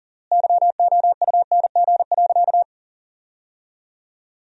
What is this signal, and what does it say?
Morse code, heard as a keyed tone, about 40 wpm, 700 Hz. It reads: YOUNG.
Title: 3x Word-Spacing - Sentences from Top 400 Words - 40wpm